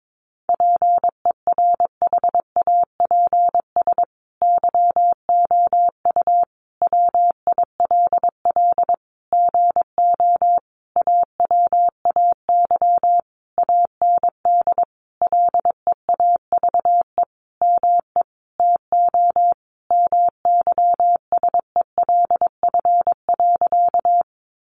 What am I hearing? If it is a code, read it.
Morse code, heard as a keyed tone, about 22 wpm, 700 Hz. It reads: PERHAPS YOU WILL GO AWAY AND LEAVE ME TO MYSELF.
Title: Sentences from Top 1300 Words 22wpm